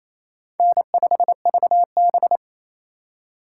Morse code, heard as a keyed tone, about 28 wpm, 700 Hz. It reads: N5VB
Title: USA Call Signs - 4 Characters - 2x Character Spacing - 28wpm